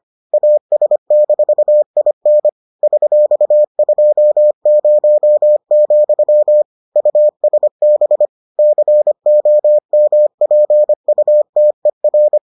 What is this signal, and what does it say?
Morse code, heard as a keyed tone, about 25 wpm, 600 Hz. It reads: AS-IN $20, USB COMPUTER